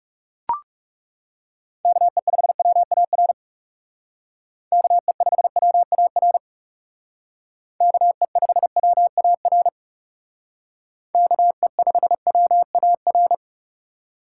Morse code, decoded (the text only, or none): KE5WAR KE5WAR KE5WAR KE5WAR